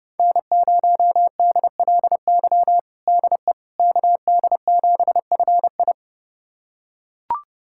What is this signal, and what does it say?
Morse code, heard as a keyed tone, about 32 wpm, 700 Hz. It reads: N0DLY DE KD7FI
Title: USA Callsign DE Callsign - 5 Characters - 30wpm